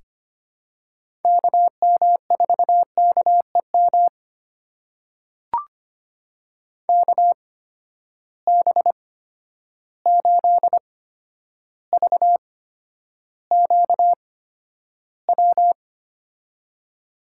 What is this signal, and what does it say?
Morse code, heard as a keyed tone, about 25 wpm, spaced as for 11 wpm, 700 Hz. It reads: KM4KEM KB8VQW
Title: USA Call Signs - 6 Characters - 8x Character Spacing - 25wpm